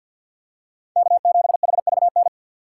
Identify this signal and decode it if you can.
Morse code, heard as a keyed tone, about 52 wpm, 700 Hz. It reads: K7HVN
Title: USA Call Signs - 5 Characters - 3x Character Spacing - 50wpm